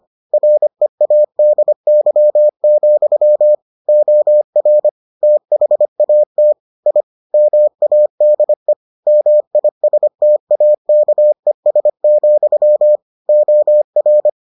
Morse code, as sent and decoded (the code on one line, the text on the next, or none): .-. .- -.. -.-- --..-- / --- .-. / - .... .- - / .. / -- .- -.. / -- .. ... - .- -.- . ... --..-- / --- .-.
READY, OR THAT I MADE MISTAKES, OR